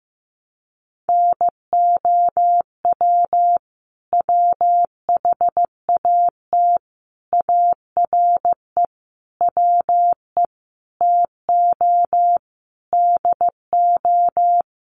Morse code, heard as a keyed tone, about 15 wpm, 700 Hz. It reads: NOW WHAT ARE WE TO DO
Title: Sentences from Top 100 Words 15wpm